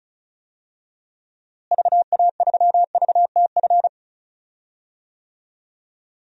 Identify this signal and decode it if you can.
Morse code, heard as a keyed tone, about 36 wpm, 700 Hz. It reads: VA3VTF
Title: DXCC-100 Call Signs - 6 Characters - Repeated 2x - 35wpm